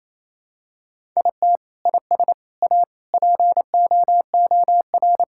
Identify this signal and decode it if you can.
Morse code, heard as a keyed tone, about 28 wpm, 700 Hz. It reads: IT IS A POOR
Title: Sentences from Top 1000 Words - Review Part 5 - 28wpm